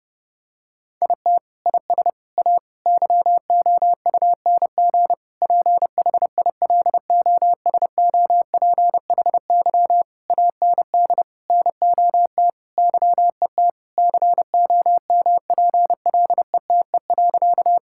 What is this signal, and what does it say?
Morse code, heard as a keyed tone, about 30 wpm, 700 Hz. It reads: IT IS A YOUNG PHILOSOPHY AND NOT YET COMPLETE.
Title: Sentences from Top 1000 Words - Review Part 6 - 30wpm